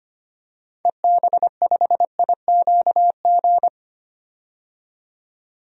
Morse code, heard as a keyed tone, about 25 wpm, 700 Hz. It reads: EB5IQG
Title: DXCC-100 Call Signs - 6 Characters - 25wpm